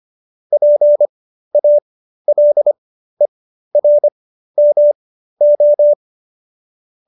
Morse code, heard as a keyed tone, about 25 wpm, spaced as for 13 wpm, 590 Hz. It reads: PALERMO
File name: qso215b 13